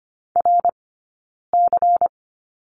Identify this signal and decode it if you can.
Morse code, heard as a keyed tone, about 25 wpm, spaced as for 9 wpm, 690 Hz. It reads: RC
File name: course / 25:10wpm / 092 Lesson 16d - Letter c - Two Characters 25wpm